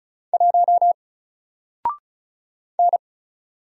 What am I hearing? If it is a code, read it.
Morse code, heard as a keyed tone, about 34 wpm, 700 Hz. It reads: 1 N
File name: Single-Letter-Number-35wpm